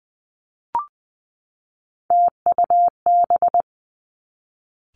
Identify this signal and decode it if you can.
Morse code, heard as a keyed tone, about 20 wpm, 700 Hz. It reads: TUB
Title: Three Challenging Letter-Number Combinations 20wpm